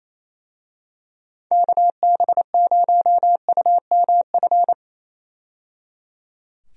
Morse code, heard as a keyed tone, about 28 wpm, 700 Hz. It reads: KB0UMF
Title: USA Call Signs - 6 Characters - 28wpm